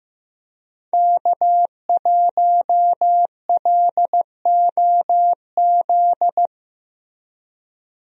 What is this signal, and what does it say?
Morse code, heard as a keyed tone, about 15 wpm, 700 Hz. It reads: K1LOZ
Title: USA Call Signs - 5 Characters - Repeated 3x - 15wpm